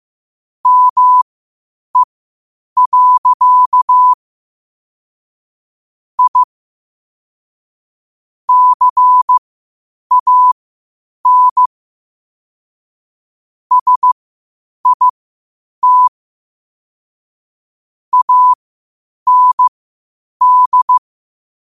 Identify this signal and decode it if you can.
Morse code, heard as a keyed tone, about 15 wpm, spaced as for 8 wpm, 1000 Hz. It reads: ME. I CAN SIT AND
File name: qotdfiles / 8wpm